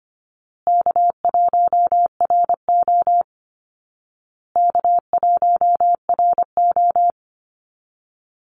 Morse code, heard as a keyed tone, about 25 wpm, 700 Hz. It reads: K1RO K1RO
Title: Easy Call Signs Repeated 3x - 25wpm